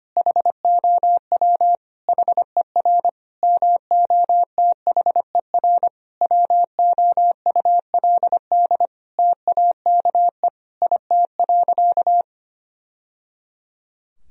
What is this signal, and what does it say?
Morse code, heard as a keyed tone, about 25 wpm, 700 Hz. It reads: HOW HER MOTHER WOULD TAKE IT.
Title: Sentences from Top 1000 Words - Review Part 4 - 25wpm